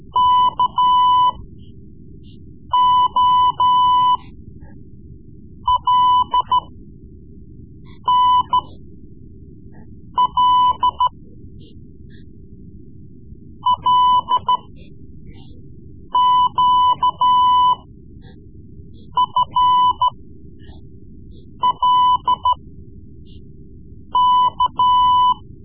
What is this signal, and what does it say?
Morse code, heard as a keyed tone, about 13 wpm, spaced as for 5 wpm, 1000 Hz. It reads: KOLNL LQFLK
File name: Morse Lessons / Morse Lessons / Tape-3B